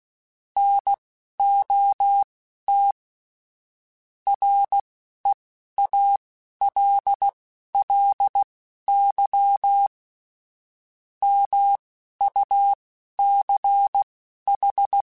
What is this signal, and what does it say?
Morse code, heard as a keyed tone, about 16 wpm, spaced as for 12 wpm, 800 Hz. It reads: NOT REALLY MUCH
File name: QSOs / QSOs-mp3s / WD0BGP 1@15